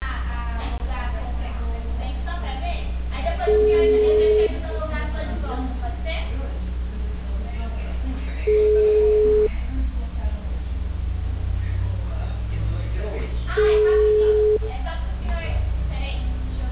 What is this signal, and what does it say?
Ringback tone (Europe, 425 Hz), with speech.